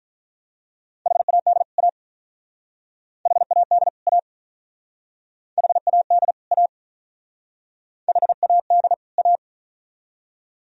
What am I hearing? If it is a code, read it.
Morse code, heard as a keyed tone, about 46 wpm, 700 Hz. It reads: HAD A HAD A HAD A HAD A